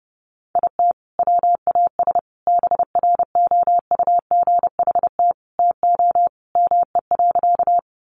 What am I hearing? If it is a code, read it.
Morse code, heard as a keyed tone, about 30 wpm, 700 Hz. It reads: IT WAS BROUGHT TO ME.
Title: Sentences from Top 500 Words 30wpm